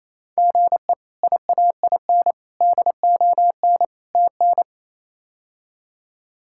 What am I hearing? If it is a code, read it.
Morse code, heard as a keyed tone, about 28 wpm, 700 Hz. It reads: GE IAIN DON TN
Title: K1USN SST QSO Practice - 28wpm